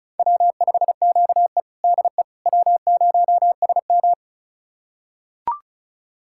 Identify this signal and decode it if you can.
Morse code, heard as a keyed tone, about 36 wpm, 700 Hz. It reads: W5QE DE W0SM
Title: USA Callsign DE Callsign - 4 Characters - 35wpm